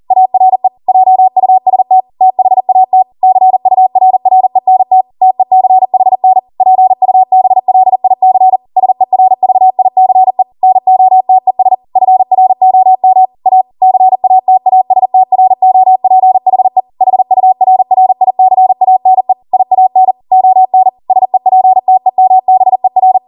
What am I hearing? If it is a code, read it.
Morse code, heard as a keyed tone, about 40 wpm, 750 Hz. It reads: ARE JUST THAT CURRENT TECHN PUBLIC SERVICE NOTES FROM A CATASTROPHE HURRICANE IAN ON SEPTEMBER